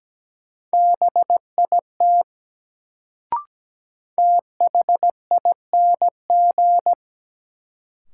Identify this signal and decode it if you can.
Morse code, heard as a keyed tone, about 17 wpm, 700 Hz. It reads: BIT THING
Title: Sets of 3 Words - New Word Drill - 17wpm